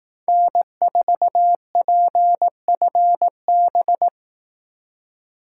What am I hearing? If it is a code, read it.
Morse code, heard as a keyed tone, about 18 wpm, 700 Hz. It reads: N4PFB
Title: Speed Racing - USA Call Signs 15wpm